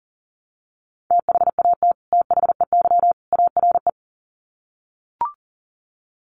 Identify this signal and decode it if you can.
Morse code, heard as a keyed tone, about 40 wpm, 700 Hz. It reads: THAT THEY ARE
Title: Sets of 3 Words 40wpm